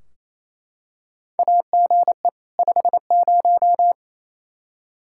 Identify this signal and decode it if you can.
Morse code, heard as a keyed tone, about 28 wpm, 700 Hz. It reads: AGE 50
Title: K1USN SST QSO Practice 28wpm